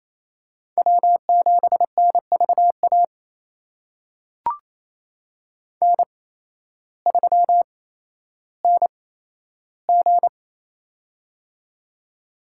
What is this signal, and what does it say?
Morse code, heard as a keyed tone, about 28 wpm, spaced as for 13 wpm, 700 Hz. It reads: W7NVA N3NG
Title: USA Call Signs - 8x Character Spacing - 28wpm